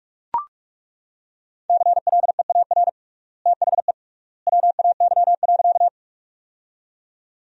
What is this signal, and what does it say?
Morse code, heard as a keyed tone, about 42 wpm, 700 Hz. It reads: CLEAR THE WAY.
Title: 3x Word-Spacing - Sentences from Top 500 Words - 45wpm